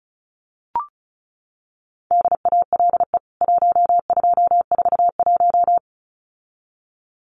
Morse code, heard as a keyed tone, about 34 wpm, 700 Hz. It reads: DALE 1241
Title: CWT Exchange 35wpm